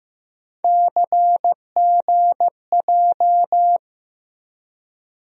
Morse code, heard as a keyed tone, about 15 wpm, 700 Hz. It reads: CGJ
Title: USA Call Sign Postfixes - 3 Letters 15wpm